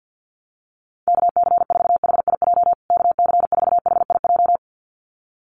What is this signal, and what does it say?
Morse code, heard as a keyed tone, about 50 wpm, 700 Hz. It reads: KC4HIJ KC4HIJ
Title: USA Call Signs Repeated 2x 50wpm